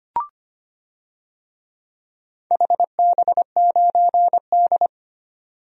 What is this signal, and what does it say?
Morse code, heard as a keyed tone, about 25 wpm, 700 Hz. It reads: HB9D
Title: DXCC-100 Call Signs - 4 Characters - 25wpm